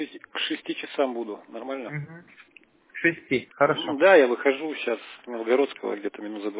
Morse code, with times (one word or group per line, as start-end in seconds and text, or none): none